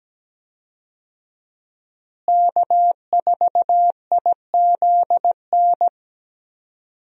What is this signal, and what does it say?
Morse code, heard as a keyed tone, about 17 wpm, 700 Hz. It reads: K4IZN
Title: USA Call Signs - 5 Characters - Repeated 2x - 17wpm